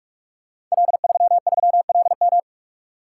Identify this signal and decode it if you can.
Morse code, heard as a keyed tone, about 44 wpm, 700 Hz. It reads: L33LM
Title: Code Groups - 5 Letter:Numbers - 45wpm